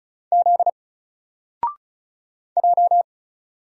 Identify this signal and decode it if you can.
Morse code, heard as a keyed tone, about 36 wpm, 700 Hz. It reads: Z J
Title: Single Letter-Number 35wpm